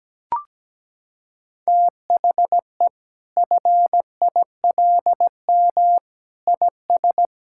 Morse code, heard as a keyed tone, about 17 wpm, 700 Hz. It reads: THE FILM IS
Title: Sentences from Top 1000 Words - Review Part 6 - 17wpm